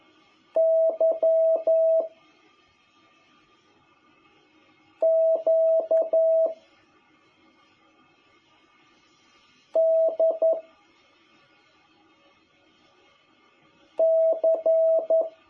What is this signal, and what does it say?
Morse code, heard as a keyed tone, about 11 wpm, 660 Hz. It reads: Y Q D C
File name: jh8cbh / 2021cw